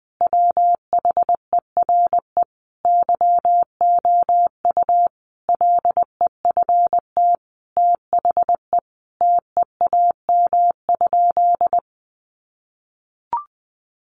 Morse code, heard as a keyed tone, about 19 wpm, 700 Hz. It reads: WHERE YOU LEFT THE TEAM?
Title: Sentences from Top 1500 Words 20wpm